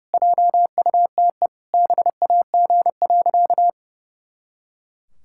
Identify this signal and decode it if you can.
Morse code, heard as a keyed tone, about 30 wpm, 700 Hz. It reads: JUTE BAG.
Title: Pangram Challenge v2 - 30wpm